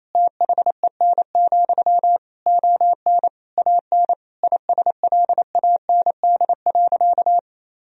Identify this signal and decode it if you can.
Morse code, heard as a keyed tone, about 28 wpm, 700 Hz. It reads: THEN, ON AN ISLAND.
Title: Sentences from Top 1000 Words - Review Part 3 - 28wpm